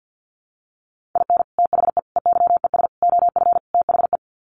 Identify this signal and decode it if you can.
Morse code, heard as a keyed tone, about 50 wpm, 700 Hz. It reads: IN THE EYES OF THE
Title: Sets of 5 Words 50wpm